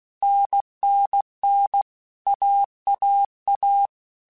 Morse code, heard as a keyed tone, about 16 wpm, 800 Hz. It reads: NNN AAA
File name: Basic Letter Groups By7Groups / AWUNDG / AWUNDG@15